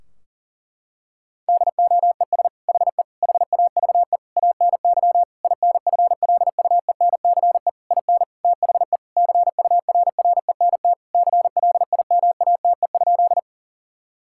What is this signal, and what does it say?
Morse code, heard as a keyed tone, about 40 wpm, 700 Hz. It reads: DOES HE HAVE ANY INFLUENCE IN THE CURRENT CLIMATE?
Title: Sentences from Top 1600 Words 40wpm